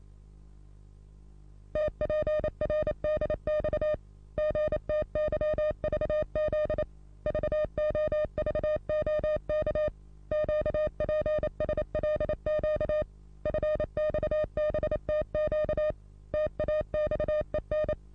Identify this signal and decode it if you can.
Morse code, heard as a keyed tone, about 29 wpm, 610 Hz. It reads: TPRDX GTYVZ VOVOK QPSLQ FXBTQ TAXEN
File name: repos / Morsekurs / DJ4UF Morsen27